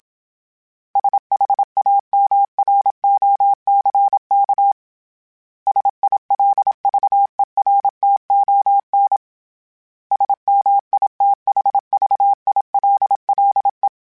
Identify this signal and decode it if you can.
Morse code, heard as a keyed tone, about 27 wpm, 800 Hz. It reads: SHAMROCK SILVERTON SMITHVILLE
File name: fromSMs / FromSMs mp3s / Cities@25